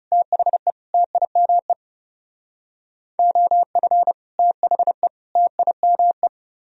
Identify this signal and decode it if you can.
Morse code, heard as a keyed tone, about 37 wpm, 700 Hz. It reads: THE TIME OF THE TIME
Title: Speed-Racing - Sets of 3 Words 30wpm